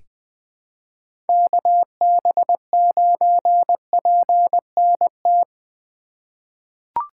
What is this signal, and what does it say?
Morse code, heard as a keyed tone, about 21 wpm, 700 Hz. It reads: KB9PNT
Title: USA Call Signs - 2x Character Spacing - 20wpm